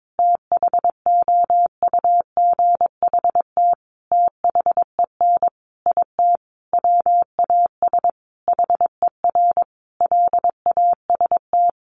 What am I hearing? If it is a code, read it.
Morse code, heard as a keyed tone, about 22 wpm, 700 Hz. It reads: THOUGHT THEN IT WAS HER LAST
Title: Sentences from Top 1400 Words 22wpm